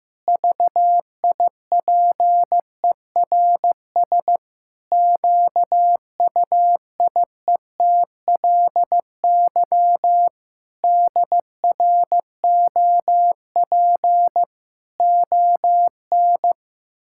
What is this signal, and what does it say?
Morse code, heard as a keyed tone, about 16 wpm, 700 Hz. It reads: VIPERS QUIETLY DROP ON
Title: Pangram Challenge v2 - 15wpm